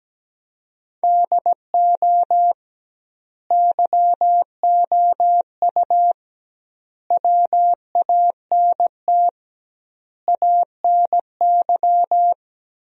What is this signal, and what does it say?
Morse code, heard as a keyed tone, about 17 wpm, 700 Hz. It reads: DO YOU WANT ANY
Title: 2x Word-Spacing - Sentences from Top 400 Words - 17wpm